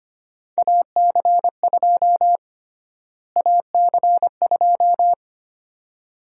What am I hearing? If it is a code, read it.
Morse code, heard as a keyed tone, about 25 wpm, 700 Hz. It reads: AC2 AC2
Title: USA Call Sign Prefixes - Repeated 2x - 25wpm